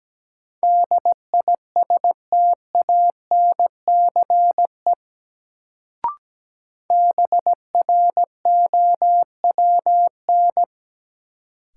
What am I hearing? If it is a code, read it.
Morse code, heard as a keyed tone, about 17 wpm, 700 Hz. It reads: DISTANCE BROWN
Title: Sentences from Top 1200 Words - New Word Drill Alt - 17wpm